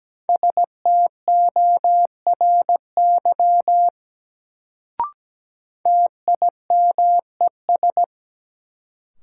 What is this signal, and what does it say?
Morse code, heard as a keyed tone, about 17 wpm, 700 Hz. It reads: STORY TIMES
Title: Sets of 3 Words - New Word Drill - 17wpm